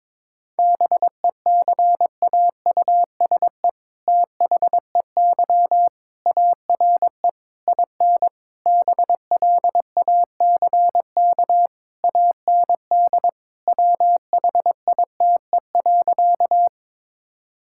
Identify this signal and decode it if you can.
Morse code, heard as a keyed tone, about 22 wpm, 690 Hz. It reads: BECAUSE THEY ARE IN BLACK AND WHITE.
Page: morse-code-ninja-practice-22wpm